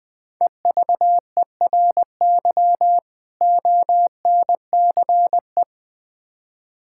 Morse code, heard as a keyed tone, about 20 wpm, 700 Hz. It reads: EVERY ONCE
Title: Build A Phrase - 5-Word Phrases - 20wpm